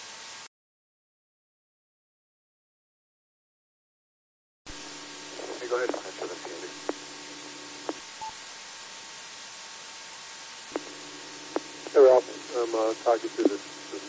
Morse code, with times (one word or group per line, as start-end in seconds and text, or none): none